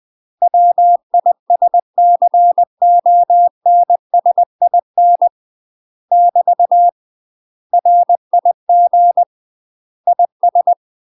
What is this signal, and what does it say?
Morse code, heard as a keyed tone, about 20 wpm, 700 Hz. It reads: WISCONSIN <BT> RIG IS
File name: UK MASTER QSOs TRAINING / 20WPM / UK MASTER QSOs 0066